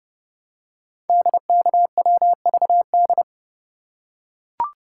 Morse code, heard as a keyed tone, about 30 wpm, 700 Hz. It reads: DKWVD
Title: Code Groups - 5 Letters - 30wpm